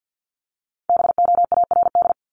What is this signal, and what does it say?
Morse code, heard as a keyed tone, about 50 wpm, 700 Hz. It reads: BOARD
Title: Top 501-750 Words 50wpm